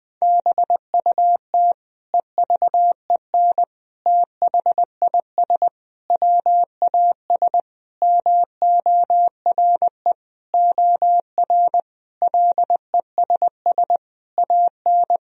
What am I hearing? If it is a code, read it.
Morse code, heard as a keyed tone, about 20 wpm, 700 Hz. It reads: BUT EVEN THIS WAS MORE OR LESS AN
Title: Sentences from Top 1000 Words - Review Part 2 - 20wpm